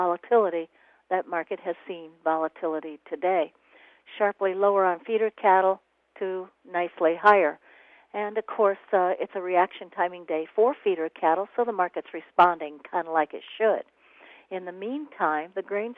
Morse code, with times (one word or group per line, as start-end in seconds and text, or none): none